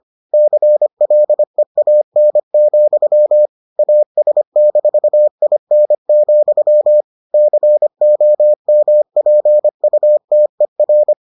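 Morse code, heard as a keyed tone, about 25 wpm, 600 Hz. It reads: CLEAN, AS-IN, COMPUTER
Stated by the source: Morse edition of the Foundations of Amateur Radio episodes.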